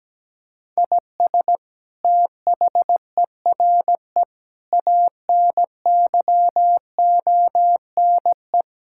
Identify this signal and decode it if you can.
Morse code, heard as a keyed tone, about 17 wpm, 700 Hz. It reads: IS THERE ANYONE
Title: Sentences from Top 1000 Words - Review Part 6 - 17wpm